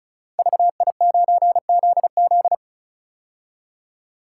Morse code, heard as a keyed tone, about 36 wpm, 700 Hz. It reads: VI9ZZ
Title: Code Groups - 5 Letter:Numbers - 35wpm